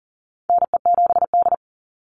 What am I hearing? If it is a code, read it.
Morse code, heard as a keyed tone, about 40 wpm, 700 Hz. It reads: NE7D
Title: Call Signs - 40wpm